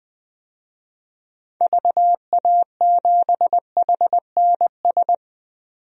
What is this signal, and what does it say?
Morse code, heard as a keyed tone, about 20 wpm, 700 Hz. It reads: VA7HNS
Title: DXCC-100 Call Signs - 6 Characters - 20wpm